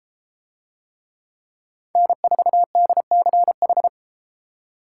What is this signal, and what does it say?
Morse code, heard as a keyed tone, about 32 wpm, 700 Hz. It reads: N4DCH